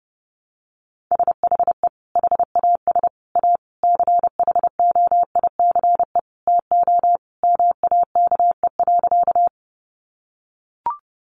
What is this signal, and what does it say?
Morse code, heard as a keyed tone, about 30 wpm, 700 Hz. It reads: SHE HAS A CHOICE TO MAKE.
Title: Sentences from Top 900 Words 30wpm